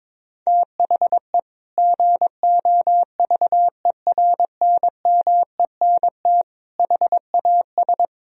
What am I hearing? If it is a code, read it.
Morse code, heard as a keyed tone, about 22 wpm, 700 Hz. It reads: THE GOVERNMENT HAS
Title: Sentences from Top 1000 Words - Review Part 3 - 22wpm